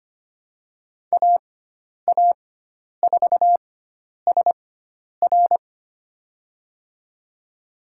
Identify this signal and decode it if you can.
Morse code, heard as a keyed tone, about 25 wpm, spaced as for 10 wpm, 700 Hz. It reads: AA4SR
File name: Lesson 16g - Letter c - Call Signs 25wpm